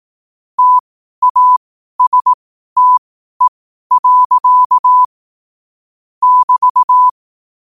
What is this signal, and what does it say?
Morse code, heard as a keyed tone, about 18 wpm, spaced as for 12 wpm, 1000 Hz. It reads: TASTE. <BT>
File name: qotdfiles / 12wpm